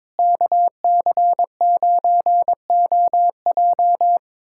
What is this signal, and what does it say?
Morse code, heard as a keyed tone, about 22 wpm, 700 Hz. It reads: KC9OJ
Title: USA Callsign DE Callsign - 5 Characters - 22wpm